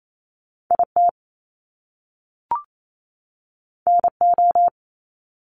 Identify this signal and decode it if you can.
Morse code, heard as a keyed tone, about 28 wpm, 700 Hz. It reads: IT NO
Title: Top Two Letter Combinations in Words 28wpm